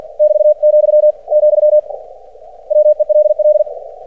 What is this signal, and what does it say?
Morse code, heard as a keyed tone, about 50 wpm, 600 Hz. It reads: <BT>, , WELL
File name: archived / CRSnet